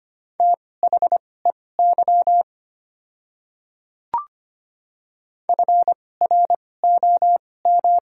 Morse code, heard as a keyed tone, about 25 wpm, spaced as for 18 wpm, 690 Hz. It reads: THEY FROM
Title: 2x Character Spacing - Top 100 Words - 25wpm